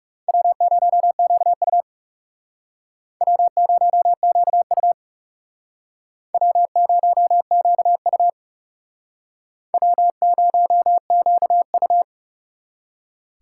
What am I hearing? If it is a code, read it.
Morse code, heard as a keyed tone, about 38 wpm, 700 Hz. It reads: W0QU W0QU W0QU W0QU